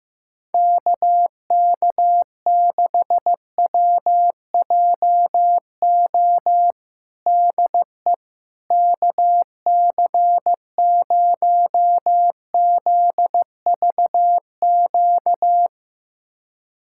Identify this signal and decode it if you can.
Morse code, heard as a keyed tone, about 15 wpm, 700 Hz. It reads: KK6WJO DE KC0ZVQ
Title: USA Callsign DE Callsign 15wpm